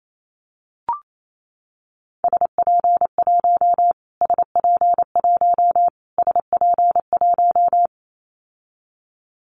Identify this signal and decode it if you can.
Morse code, heard as a keyed tone, about 28 wpm, 700 Hz. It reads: SP1 SP1 SP1
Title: Callsign Prefixes for Top 25 DXCC - Repeated 3x - 28wpm